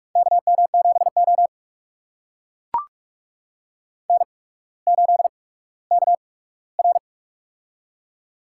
Morse code, heard as a keyed tone, about 44 wpm, spaced as for 19 wpm, 700 Hz. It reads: KM7O N8KR
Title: USA Call Signs - 4 Characters - 8x Character Spacing - 45wpm